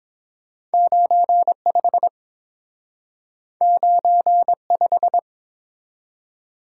Morse code, heard as a keyed tone, about 30 wpm, 700 Hz. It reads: 95 95